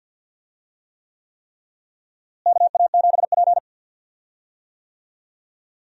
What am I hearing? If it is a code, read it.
Morse code, heard as a keyed tone, about 52 wpm, 700 Hz. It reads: KA7P
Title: USA Call Signs - 4 Characters - Repeated 3x - 50wpm